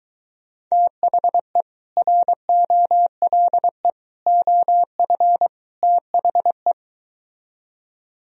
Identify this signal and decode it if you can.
Morse code, heard as a keyed tone, about 24 wpm, 700 Hz. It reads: THE ROLE OF THE